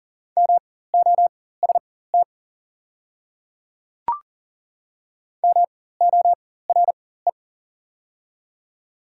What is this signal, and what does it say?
Morse code, heard as a keyed tone, about 40 wpm, spaced as for 19 wpm, 700 Hz. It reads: MOST MORE
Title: 4x Character Spacing - Top 300 Words - 40wpm